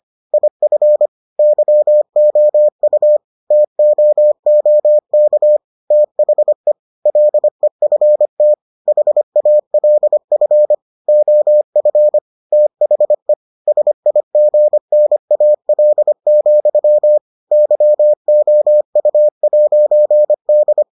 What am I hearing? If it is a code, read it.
Morse code, heard as a keyed tone, about 25 wpm, 600 Hz. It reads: IF YOU TOOK THE LEFT HALF OF THE SIGNAL, YOU'D